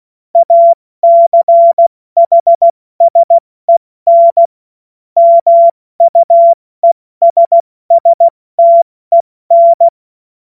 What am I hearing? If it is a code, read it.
Morse code, heard as a keyed tone, about 16 wpm, 680 Hz. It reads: ACHSEN MUESSTEN
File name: Morsekurs / Witze / Witz 21. Juni 2022 70 F80